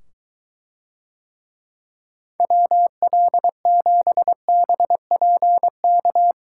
Morse code, heard as a keyed tone, about 22 wpm, 700 Hz. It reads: WL7BPK